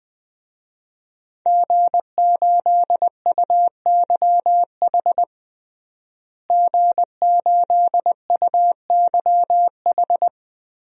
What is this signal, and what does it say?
Morse code, heard as a keyed tone, about 20 wpm, 700 Hz. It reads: G8UYH G8UYH
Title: DXCC-100 Call Signs - 5 Characters - Repeated 3x - 20wpm